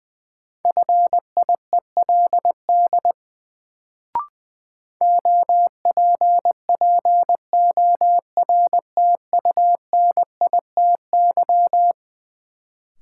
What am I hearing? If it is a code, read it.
Morse code, heard as a keyed tone, about 20 wpm, 700 Hz. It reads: FIELD OPPORTUNITY
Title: Sentences from Top 1000 Words - New Word Drill - 20wpm